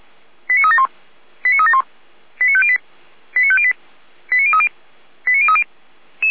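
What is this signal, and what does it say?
Five-tone selective call (ZVEI), heard as quick runs of five tones, tones of about 80 ms. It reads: 89371 89371 89508 89508 89030 89030
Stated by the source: Five-tone ZVEI selective call 89391, received by SDR, from Feuerwehr Orlamünde.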